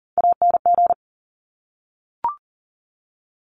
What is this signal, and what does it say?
Morse code, heard as a keyed tone, about 41 wpm, 700 Hz. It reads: ANG
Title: Top Three Letter Combinations in Words 40wpm